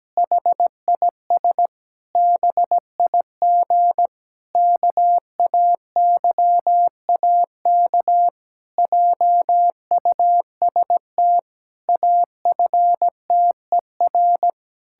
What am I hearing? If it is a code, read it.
Morse code, heard as a keyed tone, about 17 wpm, 700 Hz. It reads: HIS BIG KAYAK JUST AFTER
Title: Pangram Challenge v2 - 17wpm